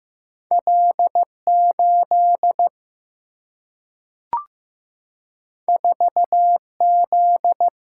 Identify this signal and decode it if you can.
Morse code, heard as a keyed tone, about 15 wpm, 700 Hz. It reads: L 8 4 Z
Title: Single-Letter Single-Number 15wpm